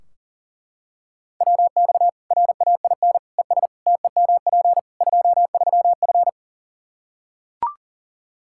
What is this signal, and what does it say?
Morse code, heard as a keyed tone, about 40 wpm, 700 Hz. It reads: WX RAIN ES TEMP 23F
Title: Practice QSO - Exchange 2 - Info - 40wpm